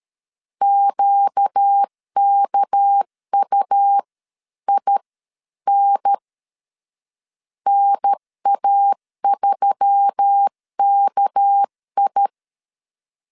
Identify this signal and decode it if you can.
Morse code, heard as a keyed tone, about 13 wpm, spaced as for 11 wpm, 800 Hz. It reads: QKUIN NA3KI
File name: CW / DET / 1 6 1500C S C 12wpm 3533